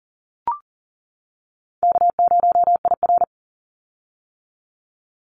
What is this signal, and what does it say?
Morse code, heard as a keyed tone, about 38 wpm, 700 Hz. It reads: K0IR
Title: Call Signs - 40wpm